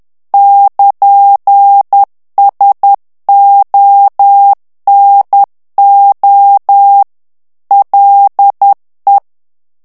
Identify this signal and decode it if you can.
Morse code, heard as a keyed tone, about 10 wpm, 800 Hz. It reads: <KN>SONO LE
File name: morse mp3 / ITALIEN texte clair 1 vitesse 10 mots minute